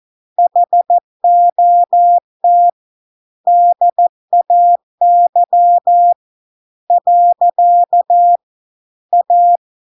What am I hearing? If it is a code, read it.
Morse code, heard as a keyed tone, about 14 wpm, 700 Hz. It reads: HOT DAY . A